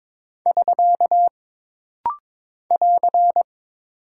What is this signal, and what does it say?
Morse code, heard as a keyed tone, about 23 wpm, 700 Hz. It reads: <SK> <AR>
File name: practice / 22wpm / Single Punctuation-Prosign - Rapid-Fire - 22wpm